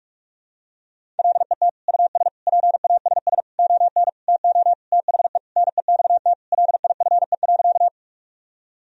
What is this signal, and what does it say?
Morse code, heard as a keyed tone, about 46 wpm, 700 Hz. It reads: LET US PASS ON TO THE NEXT LIFE.
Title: Sentences from Top 1000 Words - Review Part 1 - 45wpm